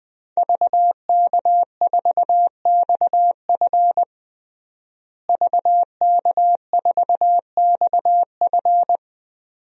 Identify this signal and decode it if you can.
Morse code, heard as a keyed tone, about 20 wpm, 700 Hz. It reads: VK4XF VK4XF
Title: DXCC-100 Call Signs - 5 Characters - Repeated 2x - 20wpm